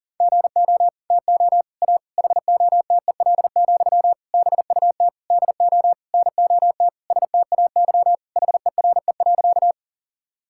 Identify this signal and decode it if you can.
Morse code, heard as a keyed tone, about 40 wpm, 700 Hz. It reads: GO TO A HOTEL, BUT DO NOT STAY HERE.
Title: Sentences from Top 1600 Words 40wpm